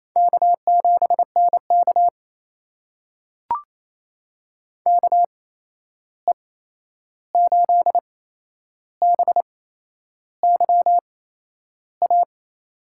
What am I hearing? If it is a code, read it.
Morse code, heard as a keyed tone, about 28 wpm, spaced as for 11 wpm, 700 Hz. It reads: K7NK KE8BYA
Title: USA Call Signs - 8x Character Spacing - 28wpm